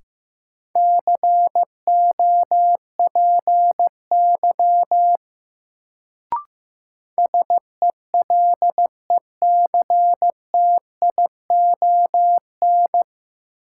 Morse code, heard as a keyed tone, about 15 wpm, 700 Hz. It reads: COPY SELECTION
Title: Sentences from Top 1400 Words - New Word Drill Alt - 15wpm